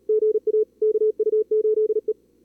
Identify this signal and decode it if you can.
Morse code, heard as a keyed tone, about 38 wpm, 440 Hz. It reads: GA KU8E
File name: CWSprint / Feb2022 / k5tr-feb-na-sprint-cw-k5ot -2300